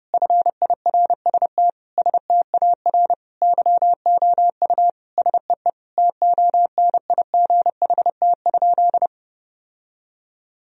Morse code, heard as a keyed tone, about 30 wpm, 700 Hz. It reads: FIRST STAR YOU SEE TONIGHT?
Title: Sentences from Top 1000 Words - Review Part 2 - 30wpm